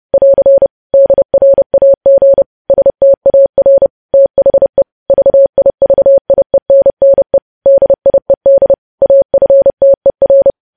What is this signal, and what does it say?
Morse code, heard as a keyed tone, about 30 wpm, 550 Hz. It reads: <AR> DRAG STAR THE VIVIENNE DIED AFTER
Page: news headlines in morse code at 30 wpm